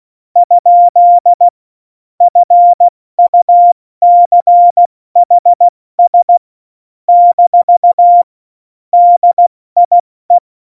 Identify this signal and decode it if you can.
Morse code, heard as a keyed tone, about 16 wpm, 700 Hz. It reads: ? FUCHS - DIE
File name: Morsekurs / Witze / Witz 28. August 2018, 70 F80